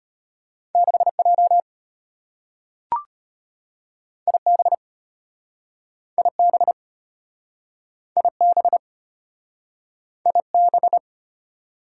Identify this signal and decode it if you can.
Morse code, heard as a keyed tone, about 33 wpm, 700 Hz. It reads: 6J IB IB IB IB